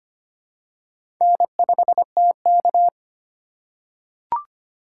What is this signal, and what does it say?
Morse code, heard as a keyed tone, about 25 wpm, 700 Hz. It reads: N5TK
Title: USA Call Signs - 4 Characters - 2x Character Spacing - 25wpm